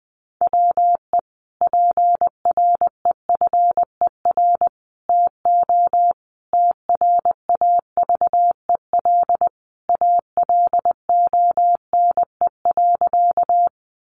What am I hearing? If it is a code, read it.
Morse code, heard as a keyed tone, about 20 wpm, 700 Hz. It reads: WE PREFER TO TRAVEL ALONE.
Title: Sentences from Top 1100 Words 20wpm